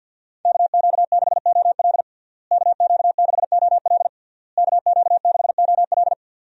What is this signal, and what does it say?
Morse code, heard as a keyed tone, about 51 wpm, 700 Hz. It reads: KQ6OL KQ6OL KQ6OL
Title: USA Call Signs - 5 Characters - Repeated 3x - 50wpm